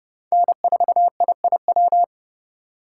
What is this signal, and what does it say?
Morse code, heard as a keyed tone, about 37 wpm, 700 Hz. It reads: N4IIW
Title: Speed Racing - USA 1x3 Call Signs 30wpm